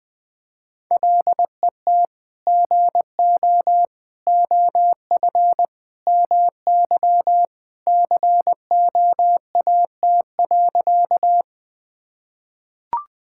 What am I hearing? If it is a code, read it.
Morse code, heard as a keyed tone, about 21 wpm, 700 Hz. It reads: LET GO OF MY COAT.
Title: Sentences from Top 1600 Words 20wpm